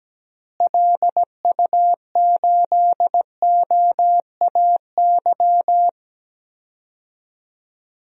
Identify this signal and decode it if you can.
Morse code, heard as a keyed tone, about 17 wpm, 700 Hz. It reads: LU8OAY
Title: DXCC-100 Call Signs - 6 Characters - Repeated 3x - 17wpm